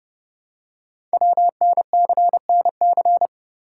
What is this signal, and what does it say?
Morse code, heard as a keyed tone, about 30 wpm, 700 Hz. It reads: WNCNC
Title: Code Groups - 5 Letters - 30wpm